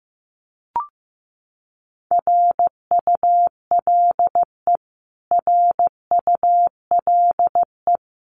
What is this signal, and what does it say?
Morse code, heard as a keyed tone, about 15 wpm, 700 Hz. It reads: RULE RULE
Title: Top 500 Words Repeated 15wpm